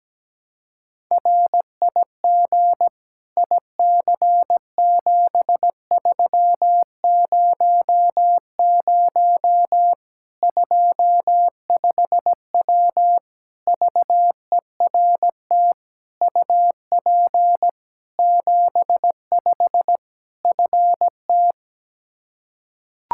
Morse code, heard as a keyed tone, about 17 wpm, 700 Hz. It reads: RIG IC7300 25W VERT UP 75 FT
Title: Practice QSO - Exchange 2 - Info - 17wpm